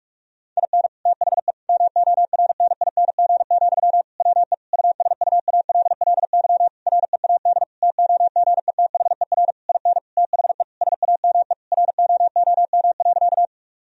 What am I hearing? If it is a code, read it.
Morse code, heard as a keyed tone, about 46 wpm, 700 Hz. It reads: IN THE MORNING, WE USUALLY READ TOGETHER IN THE SAME ROOM.
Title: Sentences from Top 1000 Words - Review Part 3 - 45wpm